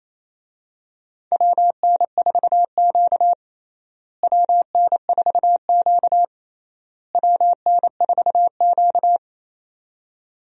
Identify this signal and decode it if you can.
Morse code, heard as a keyed tone, about 28 wpm, 700 Hz. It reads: WN4Q WN4Q WN4Q
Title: USA Call Signs - 4 Characters - Repeated 3x - 28wpm